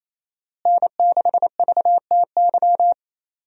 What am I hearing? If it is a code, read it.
Morse code, heard as a keyed tone, about 27 wpm, 700 Hz. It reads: N6VTY
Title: USA Call Signs - 5 Characters - Repeated 3x - 28wpm